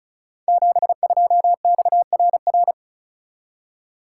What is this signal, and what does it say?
Morse code, heard as a keyed tone, about 36 wpm, 700 Hz. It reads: 72XRR
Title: Code Groups - 5 Letter:Numbers - 35wpm